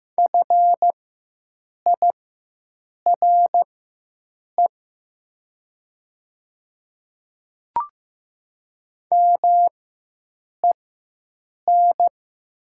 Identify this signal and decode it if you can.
Morse code, heard as a keyed tone, about 15 wpm, spaced as for 7 wpm, 690 Hz. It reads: FIRE MEN
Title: 4x Character Spacing - Top 500 Words - 15wpm